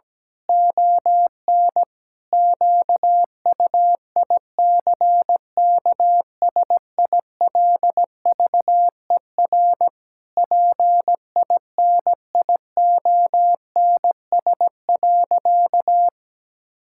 Morse code, heard as a keyed tone, about 17 wpm, 700 Hz. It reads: ON QUICKSILVER PINIONS.
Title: Pangram Challenge v2 - 17wpm